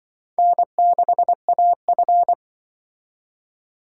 Morse code, heard as a keyed tone, about 23 wpm, 700 Hz. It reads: N6AF
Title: Speed Racing - USA 1x2 Call Signs 20wpm